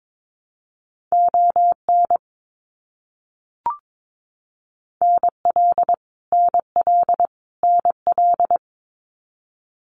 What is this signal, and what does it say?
Morse code, heard as a keyed tone, about 22 wpm, 700 Hz. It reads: ON NL NL NL
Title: Canadian Provinces and Territories - Repeated 3x - 22wpm